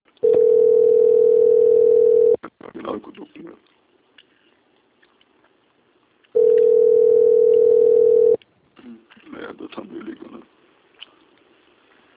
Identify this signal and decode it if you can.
Ringback tone (North America), 440 + 480 Hz, with speech.
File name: out- - - - - .73147